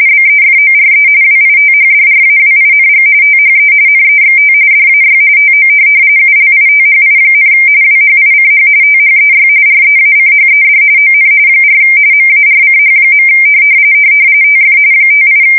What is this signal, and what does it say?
RTTY, about 45.45 baud. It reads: BEFORE WITHDRAWING THE NEXT DAY TO ULITHI, WHICH THEY REACHED ON 4 MARCH. TF 58